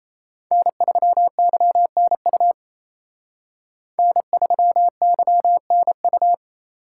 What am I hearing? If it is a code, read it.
Morse code, heard as a keyed tone, about 36 wpm, 700 Hz. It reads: N3YNU N3YNU